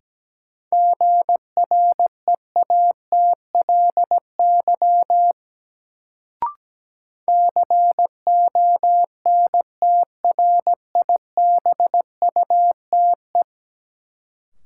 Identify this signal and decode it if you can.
Morse code, heard as a keyed tone, about 17 wpm, 700 Hz. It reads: GREATLY CONTRIBUTE
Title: Sentences from Top 1400 Words - New Word Drill Alt - 17wpm